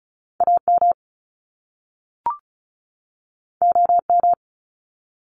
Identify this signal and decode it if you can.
Morse code, heard as a keyed tone, about 36 wpm, 700 Hz. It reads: AM OM
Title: Top Two Letter Combinations in Words 35wpm